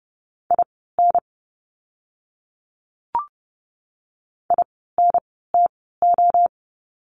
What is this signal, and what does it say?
Morse code, heard as a keyed tone, about 30 wpm, spaced as for 17 wpm, 700 Hz. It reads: IN INTO
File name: course / 30:10wpm / 237 Lesson 38f - Letter q - Words 30wpm